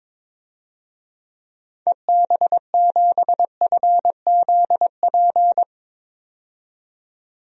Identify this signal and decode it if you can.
Morse code, heard as a keyed tone, about 22 wpm, 700 Hz. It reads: EB7FZP
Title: DXCC-100 Call Signs - 6 Characters - 22wpm